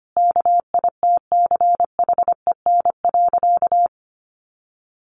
Morse code, heard as a keyed tone, about 25 wpm, 700 Hz. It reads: KITCHEN.
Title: Easy Phrases 25wpm